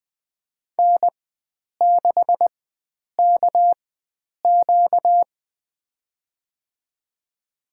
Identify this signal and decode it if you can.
Morse code, heard as a keyed tone, about 20 wpm, spaced as for 9 wpm, 700 Hz. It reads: N6KQ
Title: USA Call Signs - 4 Characters - 4x Character Spacing - 20wpm